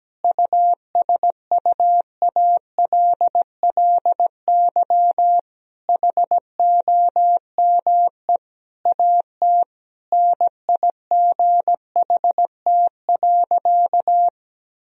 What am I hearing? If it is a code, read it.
Morse code, heard as a keyed tone, about 17 wpm, 700 Hz. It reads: USUALLY HOME AT NIGHT.
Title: Sentences from Top 1000 Words - Review Part 3 - 17wpm